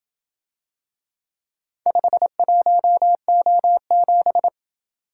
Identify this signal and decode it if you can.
Morse code, heard as a keyed tone, about 27 wpm, 700 Hz. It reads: 51O7